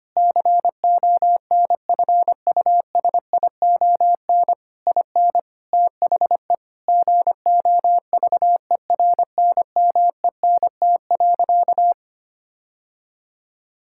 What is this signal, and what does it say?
Morse code, heard as a keyed tone, about 25 wpm, 700 Hz. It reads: CONFUSION IN THE GOVERNMENT.
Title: Sentences from Top 1600 Words 25wpm